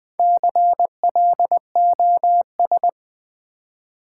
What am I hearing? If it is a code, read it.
Morse code, heard as a keyed tone, about 20 wpm, 700 Hz. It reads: CLOS
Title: Build A Word - 2 Letters At A Time - 6-Letter Words - 20wpm